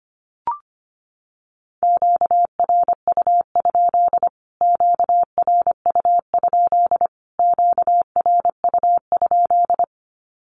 Morse code, heard as a keyed tone, about 25 wpm, 700 Hz. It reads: QRU? QRU? QRU?
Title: Q-Codes Repeated 3x 25wpm